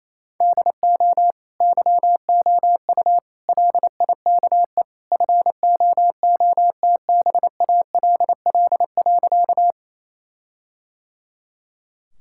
Morse code, heard as a keyed tone, about 28 wpm, 700 Hz. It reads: DO YOU LIKE FOOTBALL.
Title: Sentences from Top 1600 Words 28wpm